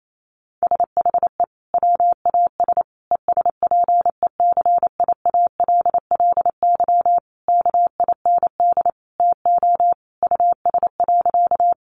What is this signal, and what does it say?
Morse code, heard as a keyed tone, about 28 wpm, 700 Hz. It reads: SHE WAS ESPECIALLY KIND TO US.
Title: Sentences from Top 700 Words 28wpm